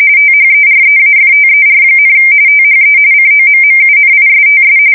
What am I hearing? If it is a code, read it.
RTTY, about 45.45 baud. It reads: THEREAFTER, THE COMMUNISTS
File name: fsk170 / USS-New-Jersey